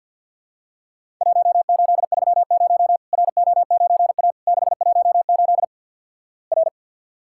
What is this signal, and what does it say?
Morse code, heard as a keyed tone, about 50 wpm, 700 Hz. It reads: 1830 RO9A 618 R
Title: Worked All Europe DX Contest - QTC Exchange - 50wpm